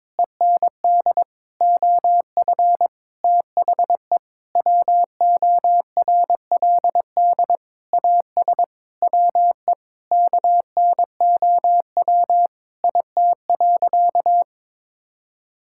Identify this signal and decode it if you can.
Morse code, heard as a keyed tone, about 22 wpm, 700 Hz. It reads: END OF THE WORLD AS WE KNOW IT.
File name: practice / 22wpm / Sentences with 8-Word Phrases - 22wpm